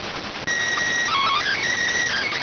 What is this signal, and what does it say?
SSTV (slow-scan television), PD 290.